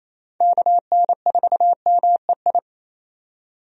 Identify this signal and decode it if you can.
Morse code, heard as a keyed tone, about 28 wpm, 700 Hz. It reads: KN4MEI
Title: USA Call Signs - 6 Characters - 28wpm